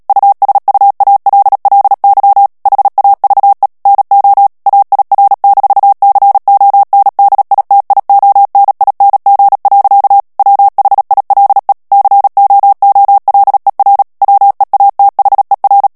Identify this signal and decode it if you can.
Morse code, heard as a keyed tone, about 37 wpm, 800 Hz. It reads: USUALLY HAVE NO AIR-CONDITIONING. WHILE COOLER WEATHER
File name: morse mp3 / ANGLAIS texte clair 2 vitesse 35 mots minute